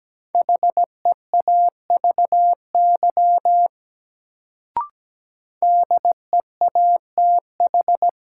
Morse code, heard as a keyed tone, about 17 wpm, 700 Hz. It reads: HEAVY DEATH
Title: Sentences from Top 1100 Words - New Word Drill Alt - 17wpm